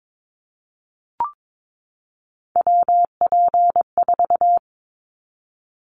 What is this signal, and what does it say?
Morse code, heard as a keyed tone, about 22 wpm, 700 Hz. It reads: WP4
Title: Callsign Prefixes for Top 25 DXCC - 22wpm